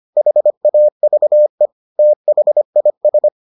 Morse code, heard as a keyed tone, about 25 wpm, 600 Hz. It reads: HAVE THIS